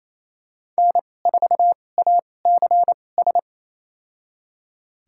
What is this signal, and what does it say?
Morse code, heard as a keyed tone, about 28 wpm, spaced as for 20 wpm, 700 Hz. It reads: N4ACS
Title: USA Call Signs - 5 Characters - 2x Character Spacing - 28wpm